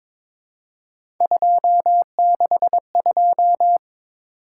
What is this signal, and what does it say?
Morse code, heard as a keyed tone, about 22 wpm, 700 Hz. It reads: 262
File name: practice / 22wpm / Three Letter-Number Combinations - 22wpm